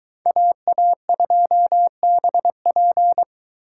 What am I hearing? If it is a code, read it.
Morse code, heard as a keyed tone, about 22 wpm, 700 Hz. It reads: AA2BP